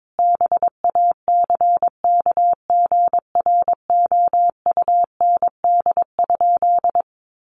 Morse code, heard as a keyed tone, about 22 wpm, 700 Hz. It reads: BACKGROUND?
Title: Sentences from Top 1300 Words 22wpm